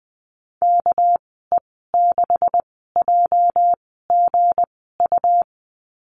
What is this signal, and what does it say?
Morse code, heard as a keyed tone, about 20 wpm, spaced as for 15 wpm, 700 Hz. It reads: KE6JGU
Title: 213 Lesson 34g - Number 8 - Call Signs 20wpm